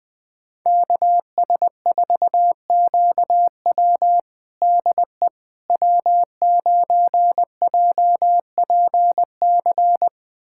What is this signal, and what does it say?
Morse code, heard as a keyed tone, about 20 wpm, 700 Hz. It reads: KS4QW DE W9JPC
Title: USA Callsign DE Callsign - 5 Characters - 20wpm